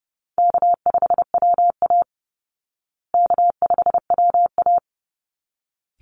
Morse code, heard as a keyed tone, about 30 wpm, 700 Hz. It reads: K5WA K5WA
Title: Easy Call Signs Repeated 2x - 30wpm